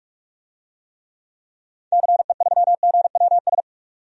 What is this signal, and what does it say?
Morse code, heard as a keyed tone, about 41 wpm, 700 Hz. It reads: CE3GWS
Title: DXCC-100 Call Signs - 6 Characters - 45wpm